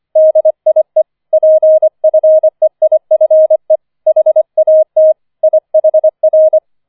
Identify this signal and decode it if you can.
Morse code, heard as a keyed tone, about 25 wpm, 630 Hz. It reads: DIE PFEIFE HAT IHR
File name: 004 Vierter Streich 25wpm